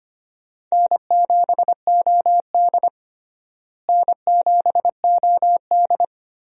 Morse code, heard as a keyed tone, about 25 wpm, 700 Hz. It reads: N7OD N7OD
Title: USA Call Signs - 4 Characters - Repeated 3x - 25wpm